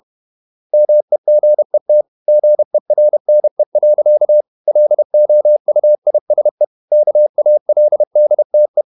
Morse code, heard as a keyed tone, about 31 wpm, 600 Hz. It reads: MEGET GERNE. LOUISE KALDTE
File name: Ferie QRQ 31wpm